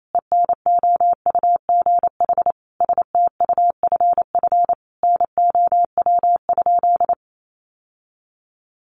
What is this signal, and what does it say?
Morse code, heard as a keyed tone, about 28 wpm, 700 Hz. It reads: ENOUGH STUFF NOW?
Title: Sentences from Top 1400 Words 28wpm